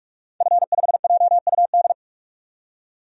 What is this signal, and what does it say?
Morse code, heard as a keyed tone, about 46 wpm, 700 Hz. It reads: F5JUD
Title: DXCC-100 Call Signs - 5 Characters - 45wpm